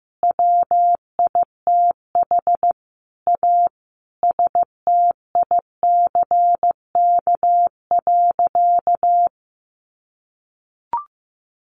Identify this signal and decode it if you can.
Morse code, heard as a keyed tone, about 14 wpm, 700 Hz. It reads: WITH A STICK.
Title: Sentences from Top 1200 Words 15wpm